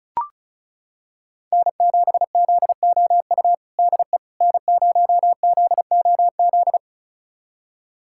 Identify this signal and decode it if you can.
Morse code, heard as a keyed tone, about 34 wpm, 690 Hz. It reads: N7ZOU DE N0ZOZ
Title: USA Callsign DE Callsign - 5 Characters - 35wpm